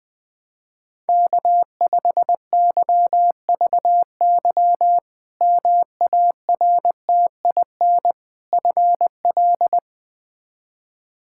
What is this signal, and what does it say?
Morse code, heard as a keyed tone, about 20 wpm, 700 Hz. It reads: K5YVY MARTIN FL
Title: K1USN SST QSO Practice - 20wpm